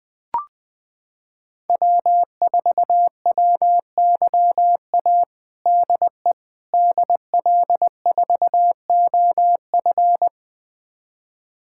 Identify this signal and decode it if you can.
Morse code, heard as a keyed tone, about 20 wpm, 700 Hz. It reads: W4WYA DE DL4OF
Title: USA Callsign DE Callsign - 5 Characters - 20wpm